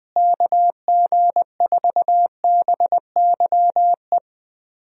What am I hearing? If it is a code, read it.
Morse code, heard as a keyed tone, about 21 wpm, 700 Hz. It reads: KG4BYE
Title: USA Call Signs - 6 Characters - Repeated 3x - 20wpm